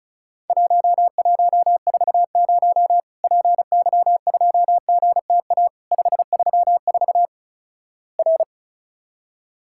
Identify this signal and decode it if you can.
Morse code, heard as a keyed tone, about 36 wpm, 700 Hz. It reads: 1140 PY2GTA 534 R
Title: Worked All Europe DX Contest - QTC Exchange - 35wpm